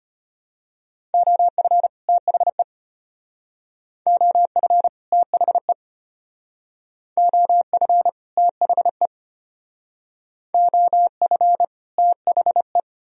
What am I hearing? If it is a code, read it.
Morse code, heard as a keyed tone, about 31 wpm, 700 Hz. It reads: OF THE OF THE OF THE OF THE